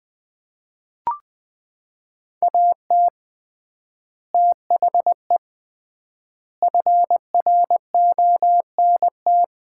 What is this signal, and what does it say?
Morse code, heard as a keyed tone, about 21 wpm, 700 Hz. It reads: AT THE FRONT
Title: 3x Word-Spacing - Sets of 5 Words - 20wpm